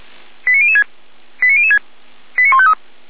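Five-tone selective call (ZVEI), heard as quick runs of five tones, tones of about 70 ms. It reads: 89006 89006 89153
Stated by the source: ZVEI five-tone call received on an SDR, recorded by the Feuerwehr Orlamünde fire brigade.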